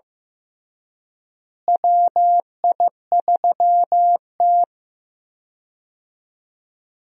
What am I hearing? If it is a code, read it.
Morse code, heard as a keyed tone, about 15 wpm, 700 Hz. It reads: WI3T
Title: USA Call Signs - 4 Characters - Repeated 3x - 15wpm